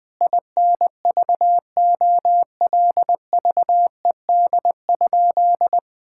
Morse code, heard as a keyed tone, about 20 wpm, 700 Hz. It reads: INVOLVED?
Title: Sentences from Top 1000 Words - Review Part 5 - 20wpm